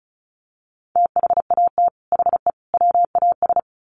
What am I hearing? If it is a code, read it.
Morse code, heard as a keyed tone, about 34 wpm, 700 Hz. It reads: THAT HE WAS
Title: Sets of 3 Words 35wpm